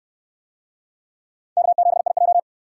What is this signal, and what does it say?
Morse code, heard as a keyed tone, about 68 wpm, 700 Hz. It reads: K7IJ